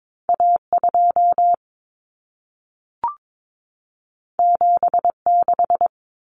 Morse code, heard as a keyed tone, about 22 wpm, 700 Hz. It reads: A2 76
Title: Common Cut Numbers - 2-Digit 22wpm